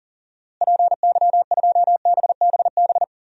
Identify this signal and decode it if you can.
Morse code, heard as a keyed tone, about 37 wpm, 700 Hz. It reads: PY2BBB
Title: DXCC-100 Call Signs - 6 Characters - Repeated 2x - 40wpm